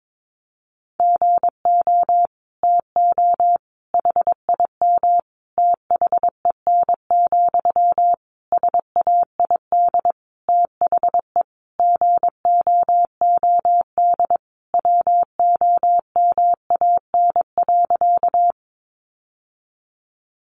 Morse code, heard as a keyed tone, about 22 wpm, 700 Hz. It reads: GO TO HIM THEN, SAID THE GOOD WOMAN.
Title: Sentences from Top 1100 Words 22wpm